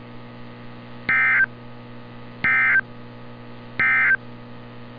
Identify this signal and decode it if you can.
Emergency Alert System (SAME) tones, heard as harsh buzzing bursts. It is heard from 1.0 to 4.2 s.
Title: NOAA Weather Radio - Altoona, PA